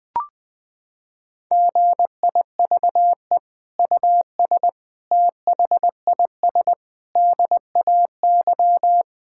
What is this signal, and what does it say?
Morse code, heard as a keyed tone, about 20 wpm, 700 Hz. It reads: GIVE US THIS DAY
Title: Sentences from Top 1600 Words 20wpm